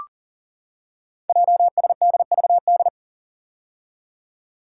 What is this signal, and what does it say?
Morse code, heard as a keyed tone, about 39 wpm, 700 Hz. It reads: JSDVD
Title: Code Groups - 5 Letters - 40wpm